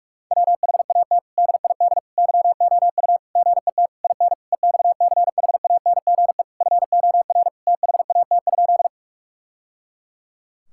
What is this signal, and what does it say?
Morse code, heard as a keyed tone, about 42 wpm, 700 Hz. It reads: WHAT DID YOU GET IN EXCHANGE FOR THAT?
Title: Sentences from Top 1000 Words - Review Part 6 - 45wpm